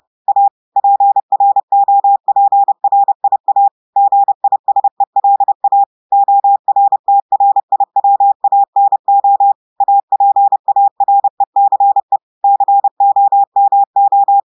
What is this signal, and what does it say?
Morse code, heard as a keyed tone, about 30 wpm, 800 Hz. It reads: A PROPRIA GISELA ORTRIWANO APARECE COMO
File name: audios / portugais / portugais texte en clair 1 vitesse 30 mots minutes